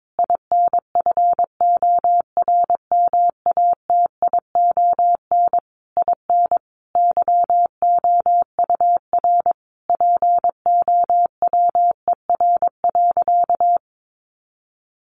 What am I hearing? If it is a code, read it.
Morse code, heard as a keyed tone, about 22 wpm, 700 Hz. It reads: INFORMATION IN YOUR POWER.
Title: Sentences from Top 600 Words 22wpm